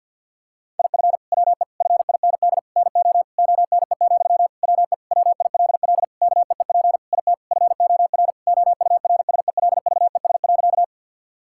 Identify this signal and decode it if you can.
Morse code, heard as a keyed tone, about 50 wpm, 700 Hz. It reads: IF WE FIND NO ONE, WE WILL KEEP IT FOR OURSELVES.
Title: Sentences from Top 1600 Words 50wpm